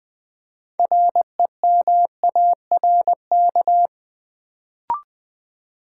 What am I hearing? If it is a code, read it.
Morse code, heard as a keyed tone, about 20 wpm, 700 Hz. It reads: REMARK
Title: Build A Word - 3 Letters At A Time - 6-Letter Words - 20wpm